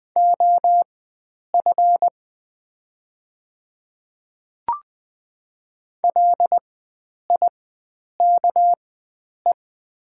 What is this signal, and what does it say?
Morse code, heard as a keyed tone, about 20 wpm, spaced as for 9 wpm, 700 Hz. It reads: OF LIKE
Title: 4x Character Spacing - Top 100 Words - 20wpm